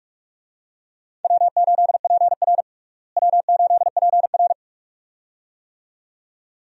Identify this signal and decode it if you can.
Morse code, heard as a keyed tone, about 42 wpm, 700 Hz. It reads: W8PR W8PR
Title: USA Call Signs - 4 Characters - Repeated 2x - 45wpm